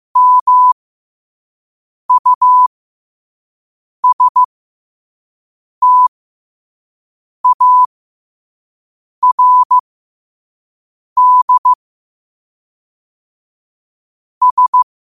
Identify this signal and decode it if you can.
Morse code, heard as a keyed tone, about 15 wpm, spaced as for 5 wpm, 1000 Hz. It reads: MUSTARD S
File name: qotdfiles / 5wpm